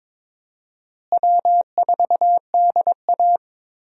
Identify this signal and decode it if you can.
Morse code, heard as a keyed tone, about 22 wpm, 700 Hz. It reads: W4DA
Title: USA Call Signs - 4 Characters - Repeated 2x - 22wpm